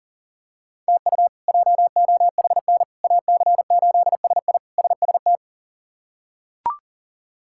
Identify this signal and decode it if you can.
Morse code, heard as a keyed tone, about 40 wpm, 700 Hz. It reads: TU JOHN AC8SI SST
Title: K1USN SST QSO Practice - 40wpm